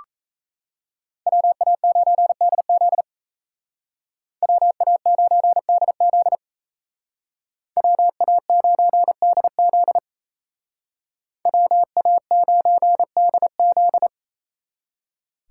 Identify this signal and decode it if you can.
Morse code, heard as a keyed tone, about 36 wpm, 700 Hz. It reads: WA9DZ WA9DZ WA9DZ WA9DZ